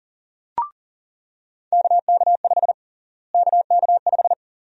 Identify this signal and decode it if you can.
Morse code, heard as a keyed tone, about 40 wpm, 700 Hz. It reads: KK5 KK5
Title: USA Call Sign Prefixes - Repeated 2x - 40wpm